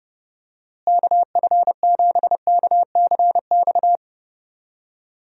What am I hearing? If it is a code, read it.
Morse code, heard as a keyed tone, about 32 wpm, 700 Hz. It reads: KF7KCX
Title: USA Call Signs - 6 Characters - Repeated 3x - 30wpm